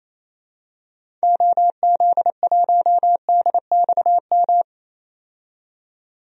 Morse code, heard as a keyed tone, about 28 wpm, 700 Hz. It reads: OZ1DXM
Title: DXCC-100 Call Signs - 6 Characters - 28wpm